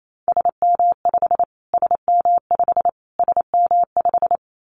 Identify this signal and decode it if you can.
Morse code, heard as a keyed tone, about 28 wpm, 700 Hz. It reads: SM5 SM5 SM5
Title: Callsign Prefixes for Top 25 DXCC - Repeated 3x - 28wpm